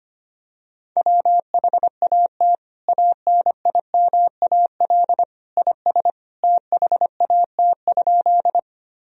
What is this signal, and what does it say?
Morse code, heard as a keyed tone, about 25 wpm, 700 Hz. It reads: WHAT ANIMAL IS THAT?
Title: Sentences from Top 1000 Words - Review Part 1 - 25wpm